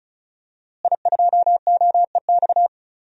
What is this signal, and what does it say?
Morse code, heard as a keyed tone, about 34 wpm, 700 Hz. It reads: I2OEX
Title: DXCC-100 Call Signs - 5 Characters - Repeated 2x - 35wpm